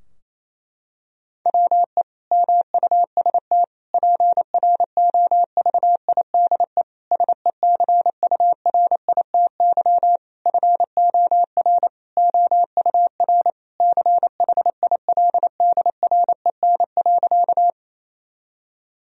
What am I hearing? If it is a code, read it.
Morse code, heard as a keyed tone, about 28 wpm, 700 Hz. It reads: WE MUST PROVIDE SECURITY FOR OUR CHILDREN.
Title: Sentences from Top 1000 Words - Review Part 4 - 28wpm